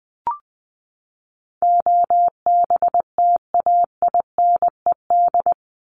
Morse code, heard as a keyed tone, about 20 wpm, 700 Hz. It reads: OBTAINED
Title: ED Suffix - Top 100 Words with Base Words 20wpm